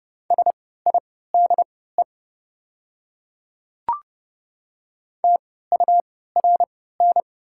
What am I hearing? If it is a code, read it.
Morse code, heard as a keyed tone, about 30 wpm, spaced as for 17 wpm, 700 Hz. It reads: SIDE TURN
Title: 3x Character Spacing - Top 200 Words - 30wpm